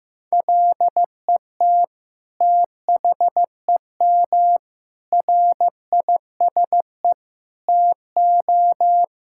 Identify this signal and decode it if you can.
Morse code, heard as a keyed tone, about 16 wpm, 700 Hz. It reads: LET THEM RISE TO
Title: Sentences from Top 1000 Words - Review Part 6 - 15wpm